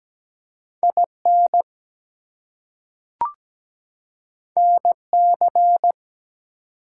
Morse code, heard as a keyed tone, about 17 wpm, 700 Hz. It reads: IN NC
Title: US States and Canadian Provinces 17wpm